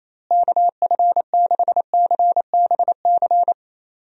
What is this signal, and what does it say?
Morse code, heard as a keyed tone, about 28 wpm, 700 Hz. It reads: KF6CBC
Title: USA Call Signs - 6 Characters - Repeated 2x - 28wpm